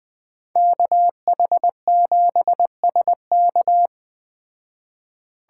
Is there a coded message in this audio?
Yes: Morse code from 0.6 to 3.9 s, heard as a keyed tone, about 20 wpm, 700 Hz. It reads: KH7SK